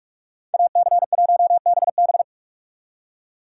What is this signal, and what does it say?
Morse code, heard as a keyed tone, about 56 wpm, 700 Hz. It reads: AC1BB
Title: Speed Racing - USA 2x2 Call Signs 45wpm